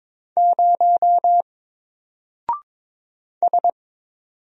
Morse code, heard as a keyed tone, about 22 wpm, 700 Hz. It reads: 0 S
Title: Single Character (Letter:Number:Punctuation:Prosign) 22wpm